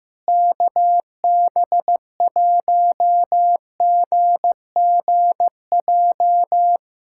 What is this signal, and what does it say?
Morse code, heard as a keyed tone, about 15 wpm, 700 Hz. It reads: KB1GGJ
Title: USA Callsign DE Callsign - 6 Characters - 15wpm